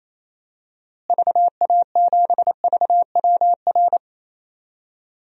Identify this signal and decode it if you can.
Morse code, heard as a keyed tone, about 27 wpm, 700 Hz. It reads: VA7VWR
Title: DXCC-100 Call Signs - 6 Characters - Repeated 2x - 28wpm